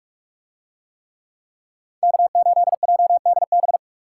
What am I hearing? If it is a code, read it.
Morse code, heard as a keyed tone, about 44 wpm, 700 Hz. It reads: K8JDB